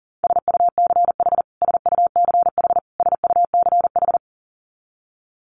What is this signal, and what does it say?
Morse code, heard as a keyed tone, about 40 wpm, 700 Hz. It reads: SUCH SUCH SUCH
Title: Top 500 Words Repeated 40wpm